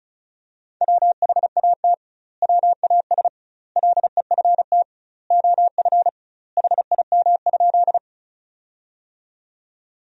Morse code, heard as a keyed tone, about 36 wpm, 700 Hz. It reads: WHAT WAS LEFT OF HIM?
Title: 2x Word-Spacing - Sentences from Top 300 Words - 35wpm